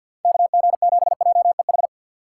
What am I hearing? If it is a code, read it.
Morse code, heard as a keyed tone, about 47 wpm, 700 Hz. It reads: KG7JEH
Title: USA Call Signs - 6 Characters - 3x Character Spacing - 50wpm